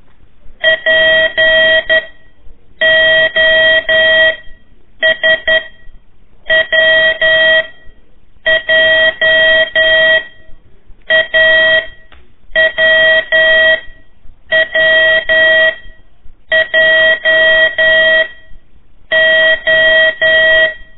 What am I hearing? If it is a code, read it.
Morse code, heard as a keyed tone, about 11 wpm, spaced as for 7 wpm, 1900 Hz. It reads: POSWJ AWWJO
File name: 3 lezione CW 1 AIR